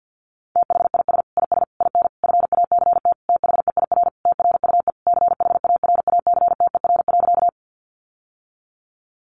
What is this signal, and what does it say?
Morse code, heard as a keyed tone, about 50 wpm, 700 Hz. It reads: THIS IS IN FACT THEIR TRUE CHARACTER.
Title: Sentences from Top 1200 Words 50wpm